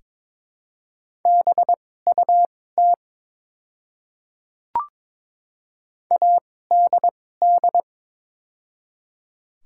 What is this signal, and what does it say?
Morse code, heard as a keyed tone, about 22 wpm, spaced as for 16 wpm, 700 Hz. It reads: BUT ADD
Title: 2x Character Spacing - Top 200 Words - 22wpm